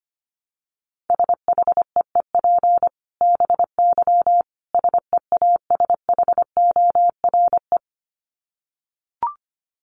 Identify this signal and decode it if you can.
Morse code, heard as a keyed tone, about 25 wpm, 700 Hz. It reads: SHEEP BY SEASHORE
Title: Dit Challenge 25wpm